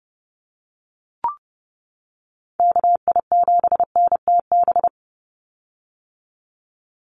Morse code, heard as a keyed tone, about 30 wpm, 700 Hz. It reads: KI7NTB
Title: USA Call Signs 30wpm